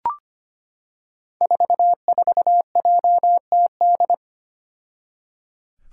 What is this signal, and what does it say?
Morse code, heard as a keyed tone, about 25 wpm, 700 Hz. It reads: 44JTD
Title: Five Challenging Letter-Number Combinations 25wpm